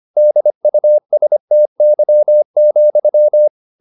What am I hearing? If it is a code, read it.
Morse code, heard as a keyed tone, about 25 wpm, 600 Hz. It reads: DUSTY,